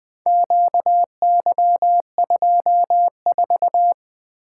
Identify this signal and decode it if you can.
Morse code, heard as a keyed tone, about 24 wpm, 700 Hz. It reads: QY24
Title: Speed-Racing - Four Letter-Number Combinations - 20wpm